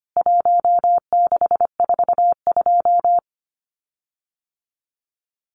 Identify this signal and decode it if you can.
Morse code, heard as a keyed tone, about 25 wpm, 700 Hz. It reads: 1642
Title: CWT Exchange 25wpm